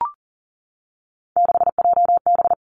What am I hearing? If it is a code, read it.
Morse code, heard as a keyed tone, about 40 wpm, 700 Hz. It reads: 6JB
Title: Three Challenging Letter-Number Combinations 40wpm